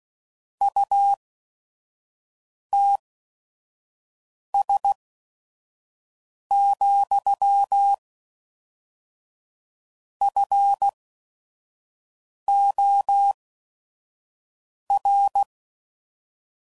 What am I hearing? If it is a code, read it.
Morse code, heard as a keyed tone, about 16 wpm, spaced as for 5 wpm, 800 Hz. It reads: UTS, FOR
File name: CW Code Groups @5WPM / Technical Subjects / AND Gate